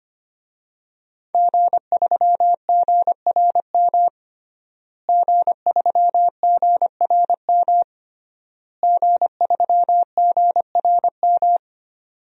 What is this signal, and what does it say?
Morse code, heard as a keyed tone, about 25 wpm, 700 Hz. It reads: G3GRM G3GRM G3GRM
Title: DXCC-100 Call Signs - 5 Characters - Repeated 3x - 25wpm